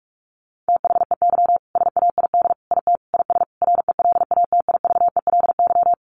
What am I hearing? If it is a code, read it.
Morse code, heard as a keyed tone, about 44 wpm, 700 Hz. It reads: THEY SAID IT IS RELATIVELY
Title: Sentences from Top 1100 Words 45wpm